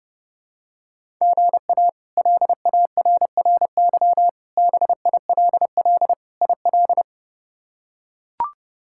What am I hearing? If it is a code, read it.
Morse code, heard as a keyed tone, about 30 wpm, 700 Hz. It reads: GA LARRY BILL IL
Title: K1USN SST QSO Practice - 30wpm